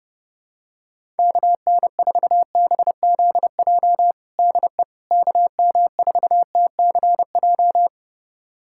Morse code, heard as a keyed tone, about 29 wpm, 700 Hz. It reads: KN4BZJ DE KM4TCJ
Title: USA Callsign DE Callsign - 6 Characters - 30wpm